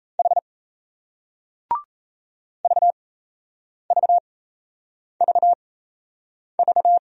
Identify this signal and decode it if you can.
Morse code, heard as a keyed tone, about 37 wpm, spaced as for 8 wpm, 700 Hz. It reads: H VVVV